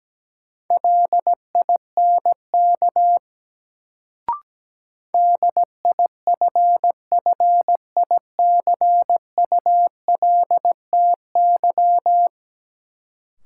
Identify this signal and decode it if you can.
Morse code, heard as a keyed tone, about 17 wpm, 700 Hz. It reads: LINK DIFFICULTY
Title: Sentences from Top 1500 Words - New Word Drill Alt - 17wpm